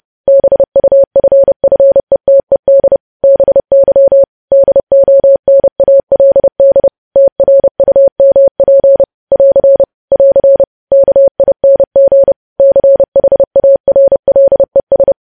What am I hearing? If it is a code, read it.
Morse code, heard as a keyed tone, about 30 wpm, 550 Hz. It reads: BUFFETED BY DONALD TRUMP <AR> <AR> KING CHARLES
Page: news headlines in morse code at 30 wpm